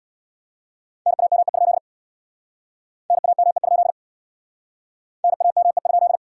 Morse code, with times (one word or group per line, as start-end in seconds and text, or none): 1.1-1.8 NAME?
3.1-3.9 NAME?
5.2-6.2 NAME?